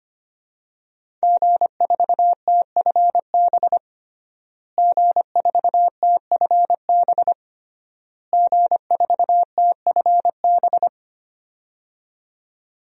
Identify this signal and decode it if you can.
Morse code, heard as a keyed tone, about 25 wpm, 700 Hz. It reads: G4TFB G4TFB G4TFB
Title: DXCC-100 Call Signs - 5 Characters - Repeated 3x - 25wpm